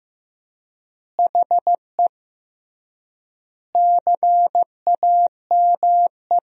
Morse code, heard as a keyed tone, about 15 wpm, 700 Hz. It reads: HE CAME
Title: 3x Word-Spacing - Sentences from Top 500 Words - 15wpm